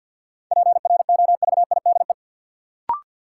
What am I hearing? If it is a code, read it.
Morse code, heard as a keyed tone, about 53 wpm, 700 Hz. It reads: PROVIDE
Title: Sentences from Top 1000 Words - Review Part 2 - New Word Drill - 50wpm